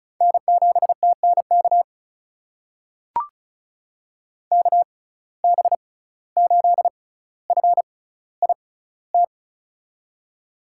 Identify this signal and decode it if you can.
Morse code, heard as a keyed tone, about 36 wpm, spaced as for 17 wpm, 700 Hz. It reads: N7TNK KB8FIT
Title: USA Call Signs - 6x Character Spacing - 35wpm